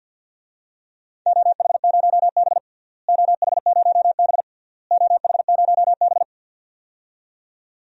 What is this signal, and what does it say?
Morse code, heard as a keyed tone, about 50 wpm, 700 Hz. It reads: OH0B OH0B OH0B
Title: DXCC-100 Call Signs - 4 Characters - Repeated 3x - 50wpm